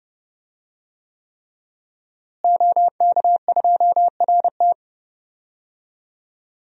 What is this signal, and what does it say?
Morse code, heard as a keyed tone, about 29 wpm, 700 Hz. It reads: OK2RT
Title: DXCC-100 Call Signs - 5 Characters - Repeated 3x - 30wpm